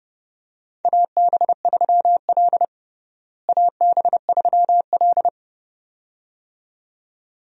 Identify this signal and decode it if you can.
Morse code, heard as a keyed tone, about 31 wpm, 700 Hz. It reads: AB3L AB3L
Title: USA Call Signs - 4 Characters - Repeated 2x - 30wpm